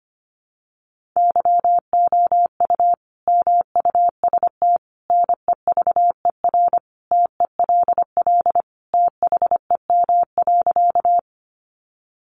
Morse code, heard as a keyed tone, about 25 wpm, 700 Hz. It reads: YOU MUST NEVER TELL THEM.
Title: Sentences from Top 900 Words 25wpm